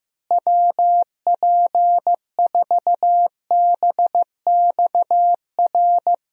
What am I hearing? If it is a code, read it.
Morse code, heard as a keyed tone, about 15 wpm, 700 Hz. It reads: WP4BXR
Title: USA Call Signs - 6 Characters - Repeated 2x - 15wpm